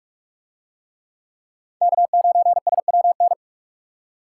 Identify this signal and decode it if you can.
Morse code, heard as a keyed tone, about 44 wpm, 700 Hz. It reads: K9SWN